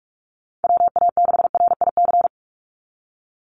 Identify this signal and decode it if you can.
Morse code, heard as a keyed tone, about 44 wpm, 700 Hz. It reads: WA6RIC
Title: NA SOTA Chaser Callsigns 45wpm